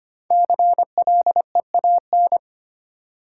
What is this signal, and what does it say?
Morse code, heard as a keyed tone, about 25 wpm, 700 Hz. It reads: CLEAN
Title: Sentences from Top 1100 Words - New Word Drill - 25wpm